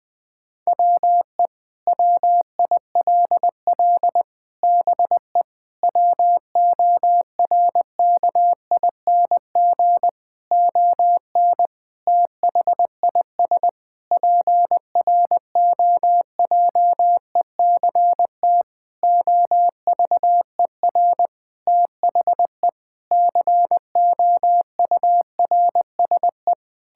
Morse code, heard as a keyed tone, about 20 wpm, 700 Hz. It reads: WE WILL BE WORKING ON THIS PROJECT OVER THE COURSE
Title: Sentences with 6-Word Phrases - 20wpm